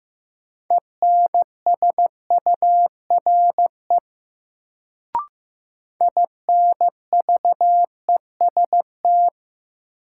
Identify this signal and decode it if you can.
Morse code, heard as a keyed tone, about 15 wpm, 700 Hz. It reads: ENSURE INVEST
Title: Sentences from Top 1100 Words - New Word Drill Alt - 15wpm